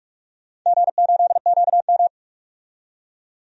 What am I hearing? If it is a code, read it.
Morse code, heard as a keyed tone, about 43 wpm, 700 Hz. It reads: G8QM
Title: DXCC-100 Call Signs - 4 Characters - 45wpm